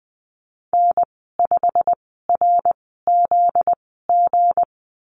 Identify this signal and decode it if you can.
Morse code, heard as a keyed tone, about 20 wpm, spaced as for 15 wpm, 700 Hz. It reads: N5RZG
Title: 245 Lesson 39g - Letter z - Call Signs 20wpm